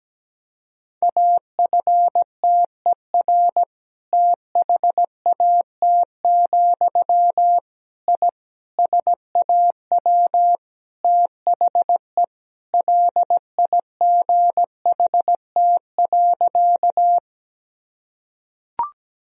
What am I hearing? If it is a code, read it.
Morse code, heard as a keyed tone, about 17 wpm, 700 Hz. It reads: AFTER THAT, I SAW THE LIGHT.
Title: Sentences from Top 1000 Words - Review Part 1 - 17wpm